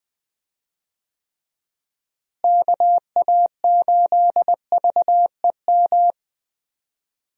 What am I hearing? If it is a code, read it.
Morse code, heard as a keyed tone, about 20 wpm, 700 Hz. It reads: KA8VEM
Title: USA Call Signs - 6 Characters - Repeated 2x - 20wpm